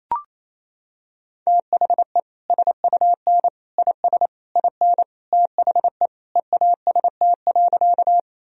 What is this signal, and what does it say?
Morse code, heard as a keyed tone, about 28 wpm, 700 Hz. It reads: THE SUN IS IN THE EAST.
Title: Sentences from Top 1000 Words - Review Part 3 - 28wpm